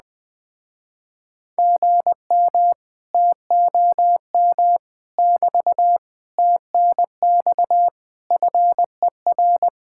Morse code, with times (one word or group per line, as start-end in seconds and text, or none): 1.6-2.7 GM
3.1-4.8 TOM
5.2-6.0 <BT>
6.4-7.9 TNX
8.3-9.7 FER